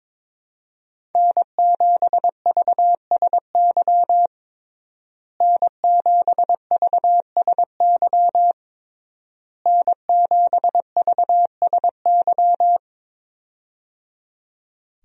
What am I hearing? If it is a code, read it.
Morse code, heard as a keyed tone, about 22 wpm, 700 Hz. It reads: N7VSY N7VSY N7VSY
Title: USA Call Signs - 5 Characters - Repeated 3x - 22wpm